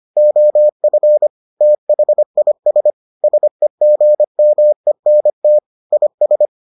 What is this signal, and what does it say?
Morse code, heard as a keyed tone, about 25 wpm, 600 Hz. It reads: OF THIS SEGMENT IS